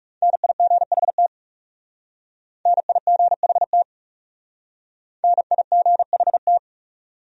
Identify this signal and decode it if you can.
Morse code, heard as a keyed tone, about 36 wpm, 700 Hz. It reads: NIGHT NIGHT NIGHT
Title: Speed-Racing Top 500 Words 30wpm